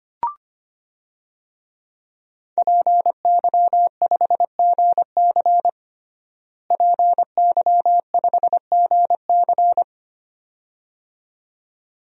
Morse code, heard as a keyed tone, about 25 wpm, 700 Hz. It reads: PY5GC PY5GC
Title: DXCC-100 Call Signs - 5 Characters - Repeated 2x - 25wpm